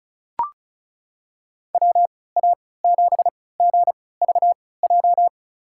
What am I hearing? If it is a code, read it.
Morse code, heard as a keyed tone, about 34 wpm, spaced as for 20 wpm, 700 Hz. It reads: WA7GVJ
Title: USA Call Signs - 6 Characters - 3x Character Spacing - 35wpm